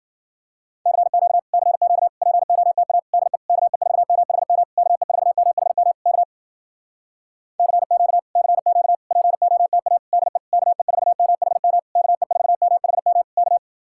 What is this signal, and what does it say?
Morse code, heard as a keyed tone, about 57 wpm, 700 Hz. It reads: CQ CQ POTA DE KE4MHM KE4MHM K CQ CQ POTA DE KE4MHM KE4MHM K